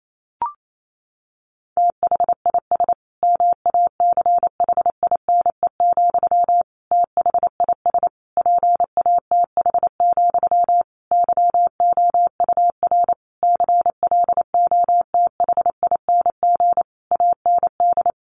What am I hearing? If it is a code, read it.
Morse code, heard as a keyed tone, about 28 wpm, 700 Hz. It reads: THIS MACHINE, THIS PATH, YOUR CLOTHING AND
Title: A Sound Of Thunder 28wpm